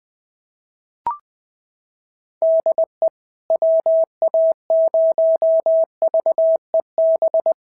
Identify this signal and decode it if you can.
Morse code, heard as a keyed tone, about 18 wpm, 650 Hz. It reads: DE WA0VEB
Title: SOTA QSO Practice 20wpm